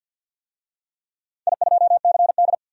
Morse code, heard as a keyed tone, about 45 wpm, 700 Hz. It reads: I2CD
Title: DXCC-100 Call Signs - 4 Characters - 50wpm